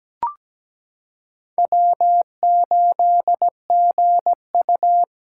Morse code, heard as a keyed tone, about 17 wpm, 700 Hz. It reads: W8GU
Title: K1USN SST QSO Practice - 17wpm